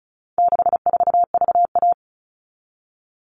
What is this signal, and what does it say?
Morse code, heard as a keyed tone, about 34 wpm, 700 Hz. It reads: 64VA
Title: All Cut Numbers - 4-Digit 35wpm